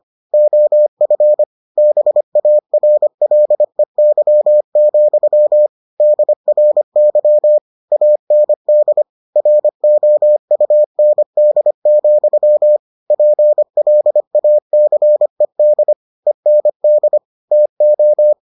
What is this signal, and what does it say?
Morse code, heard as a keyed tone, about 25 wpm, 600 Hz. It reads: OF BARLEY, DRY AND ROUND, PLACED END TO